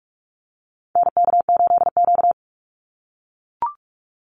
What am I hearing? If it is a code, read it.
Morse code, heard as a keyed tone, about 44 wpm, 700 Hz. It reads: NK8Q
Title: NA SOTA Activator Callsigns 45wpm